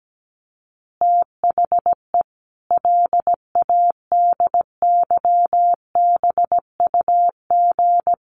Morse code, heard as a keyed tone, about 17 wpm, 700 Hz. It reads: THE LADYBUG
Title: Additional Easy Phrases 17wpm